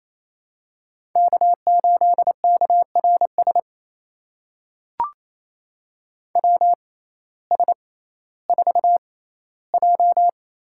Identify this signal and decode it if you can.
Morse code, heard as a keyed tone, about 28 wpm, spaced as for 15 wpm, 700 Hz. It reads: K8KRS WS4J
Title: USA Call Signs - 5 Characters - 6x Character Spacing - 28wpm